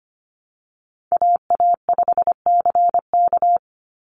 Morse code, heard as a keyed tone, about 25 wpm, 700 Hz. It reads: AA5CK
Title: NA SOTA Chaser Callsigns 25wpm